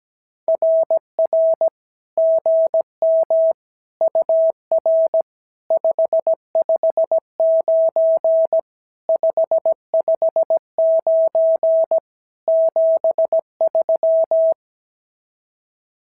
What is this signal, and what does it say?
Morse code, heard as a keyed tone, about 17 wpm, 650 Hz. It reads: RR GM UR 559 559 73
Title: SOTA QSO Practice 17wpm